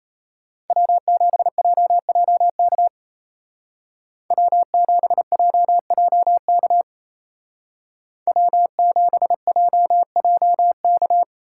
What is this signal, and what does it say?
Morse code, heard as a keyed tone, about 34 wpm, 700 Hz. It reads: W7JJK W7JJK W7JJK
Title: Speed Racing - USA 1x3 Call Signs 28wpm